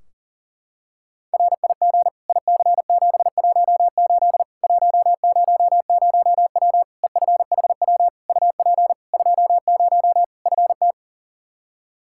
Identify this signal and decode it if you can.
Morse code, heard as a keyed tone, about 40 wpm, 700 Hz. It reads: RIG IC718 100W EFHW UP 20 FT
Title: Practice QSO - Exchange 2 - Info - 40wpm